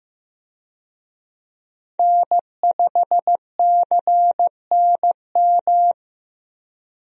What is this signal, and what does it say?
Morse code, heard as a keyed tone, about 15 wpm, 700 Hz. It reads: N5CNM
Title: USA Call Signs - 5 Characters - Repeated 2x - 15wpm